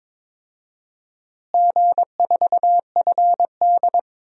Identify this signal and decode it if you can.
Morse code, heard as a keyed tone, about 22 wpm, 700 Hz. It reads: G4FD
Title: DXCC-100 Call Signs - 4 Characters - Repeated 3x - 22wpm